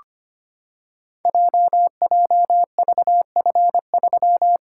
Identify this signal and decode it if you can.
Morse code, heard as a keyed tone, about 25 wpm, 700 Hz. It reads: JJVF3
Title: Code Groups - 5 Letter:Numbers - 25wpm